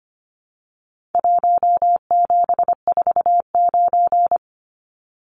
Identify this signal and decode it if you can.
Morse code, heard as a keyed tone, about 25 wpm, 700 Hz. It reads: 1749
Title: Numbers - Contest Style - 4-Digit 25wpm